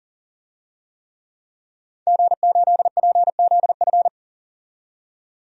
Morse code, heard as a keyed tone, about 40 wpm, 700 Hz. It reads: G8PZF
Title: DXCC-100 Call Signs - 5 Characters - 40wpm